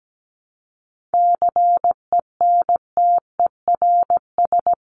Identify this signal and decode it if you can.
Morse code, heard as a keyed tone, about 17 wpm, 700 Hz. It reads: CENTERS
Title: Sets of 5 Words Encore2 17wpm